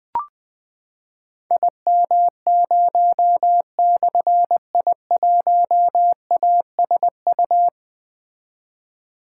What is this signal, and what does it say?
Morse code, heard as a keyed tone, about 20 wpm, 700 Hz. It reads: IM0/I1ASU
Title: Difficult Call Signs 20wpm